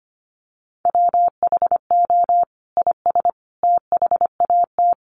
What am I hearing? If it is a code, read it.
Morse code, heard as a keyed tone, about 25 wpm, 700 Hz. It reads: WHO IS THAT
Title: Sentences from Top 200 Words 25wpm